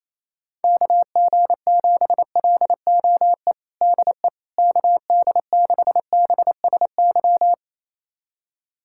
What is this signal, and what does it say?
Morse code, heard as a keyed tone, about 28 wpm, 700 Hz. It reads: KG7LOE DE KD6BSY
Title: USA Callsign DE Callsign - 6 Characters - 28wpm